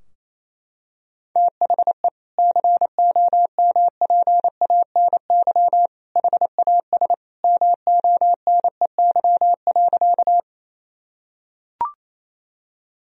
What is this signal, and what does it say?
Morse code, heard as a keyed tone, about 28 wpm, 700 Hz. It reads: THE COMPANY HAS MONEY.
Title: Sentences from Top 1000 Words - Review Part 3 - 28wpm